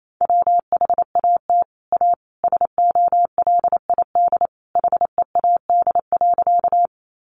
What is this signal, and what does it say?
Morse code, heard as a keyed tone, about 28 wpm, 700 Hz. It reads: WHAT A SOLID HEAD.
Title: Sentences from Top 1500 Words 28wpm